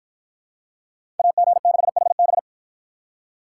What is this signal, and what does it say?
Morse code, heard as a keyed tone, about 53 wpm, 700 Hz. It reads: AG6HB